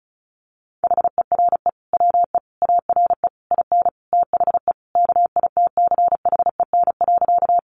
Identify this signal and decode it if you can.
Morse code, heard as a keyed tone, about 34 wpm, 700 Hz. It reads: HERE WE ARE IN THE KITCHEN.
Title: Sentences from Top 1400 Words 35wpm